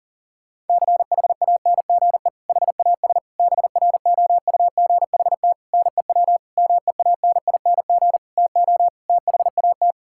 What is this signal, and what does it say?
Morse code, heard as a keyed tone, about 40 wpm, 700 Hz. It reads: CHANGE HAS BROUGHT NEW MEANING TO THAT
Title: Sentences from Top 1600 Words 40wpm